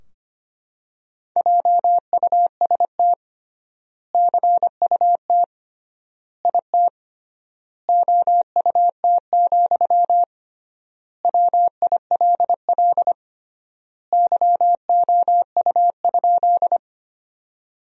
Morse code, heard as a keyed tone, about 25 wpm, 700 Hz. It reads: JUST CUT IT OUT, WILL YOU?
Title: 3x Word-Spacing - Sentences from Top 300 Words - 25wpm